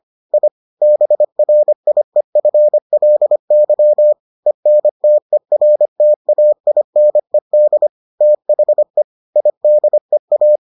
Morse code, heard as a keyed tone, about 25 wpm, 600 Hz. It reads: I BRIEFLY ENTERTAINED THE IDEA